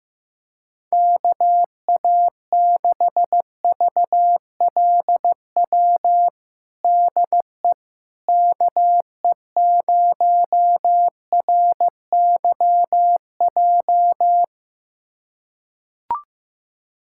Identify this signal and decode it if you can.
Morse code, heard as a keyed tone, about 15 wpm, 700 Hz. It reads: KA6VLW DE KE0RYJ
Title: USA Callsign DE Callsign - 6 Characters - 15wpm